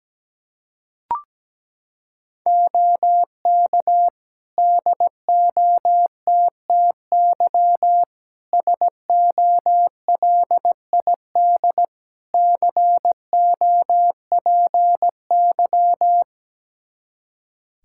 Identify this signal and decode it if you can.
Morse code, heard as a keyed tone, about 17 wpm, 700 Hz. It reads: OK DOTTY SOLID COPY
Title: K1USN SST QSO Practice 17wpm